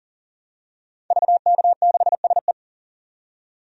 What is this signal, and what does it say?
Morse code, heard as a keyed tone, about 40 wpm, 700 Hz. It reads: VK6SE
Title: DXCC-100 Call Signs - 5 Characters - 40wpm